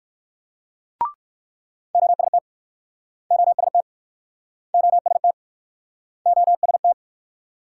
Morse code, heard as a keyed tone, about 62 wpm, 700 Hz. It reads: OST OST OST OST